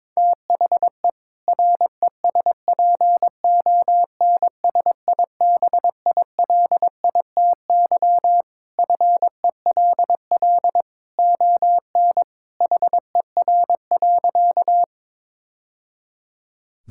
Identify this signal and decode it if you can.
Morse code, heard as a keyed tone, about 22 wpm, 700 Hz. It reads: THE RESPONSIBILITY FELL ON HER.
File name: practice / 22wpm / Sentences with 2-Word Phrases - 22wpm